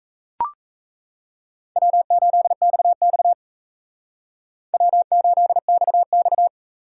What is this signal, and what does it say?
Morse code, heard as a keyed tone, about 34 wpm, 700 Hz. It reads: W8XX W8XX
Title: Speed Racing - USA Call Signs 28wpm